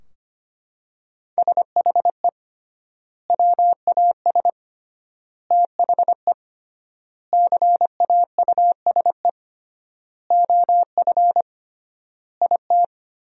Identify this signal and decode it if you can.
Morse code, heard as a keyed tone, about 25 wpm, 700 Hz. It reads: SHE WAS THE CAUSE OF IT
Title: 3x Word-Spacing - Sentences from Top 200 Words - 25wpm